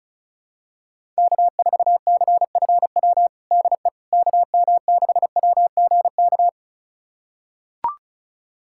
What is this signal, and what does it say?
Morse code, heard as a keyed tone, about 36 wpm, 700 Hz. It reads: K4CFW DE KM6WGK
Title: USA Callsign DE Callsign 35wpm